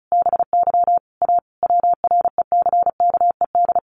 Morse code, heard as a keyed tone, about 34 wpm, 700 Hz. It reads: BY A WRECKED
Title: A Sound Of Thunder 35wpm